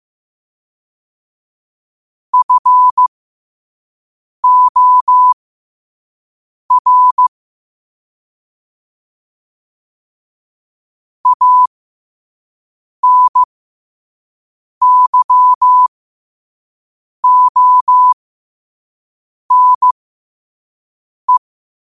Morse code, heard as a keyed tone, about 15 wpm, spaced as for 5 wpm, 1000 Hz. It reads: FOR ANYONE
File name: qotdfiles / 5wpm